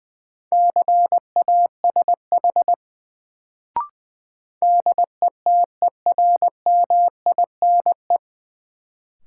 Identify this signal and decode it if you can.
Morse code, heard as a keyed tone, about 20 wpm, 700 Hz. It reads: CASH DETERMINE
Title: Sentences from Top 700 Words - New Word Drill - 20wpm